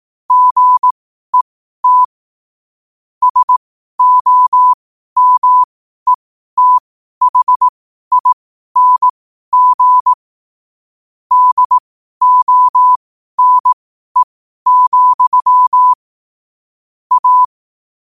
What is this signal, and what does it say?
Morse code, heard as a keyed tone, about 18 wpm, spaced as for 12 wpm, 1000 Hz. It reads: GET SOMETHING DONE, A
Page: New quotes every day in morse code at 12 Words per minute.